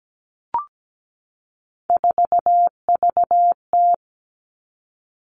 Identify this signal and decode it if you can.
Morse code, heard as a keyed tone, about 17 wpm, 700 Hz. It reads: 4VT
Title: All Cut Numbers - 3-Digit 17wpm